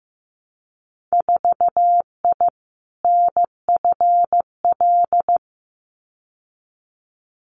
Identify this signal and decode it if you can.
Morse code, heard as a keyed tone, about 15 wpm, 700 Hz. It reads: 4I NFL
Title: Winter Field Day Exchanges 15wpm